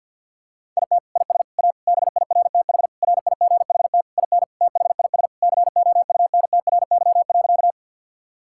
Morse code, heard as a keyed tone, about 50 wpm, 700 Hz. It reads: IT IS A BIRTH RIGHT IN THIS COUNTRY.
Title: Sentences from Top 1000 Words - Review Part 5 - 50wpm